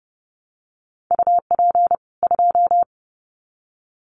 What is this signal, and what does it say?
Morse code, heard as a keyed tone, about 30 wpm, 700 Hz. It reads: UP 2
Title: Common QSO Phrases 30wpm